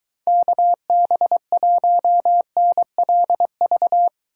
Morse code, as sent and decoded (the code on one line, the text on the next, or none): -.- -... .---- -. .-.. ...-
KB1NLV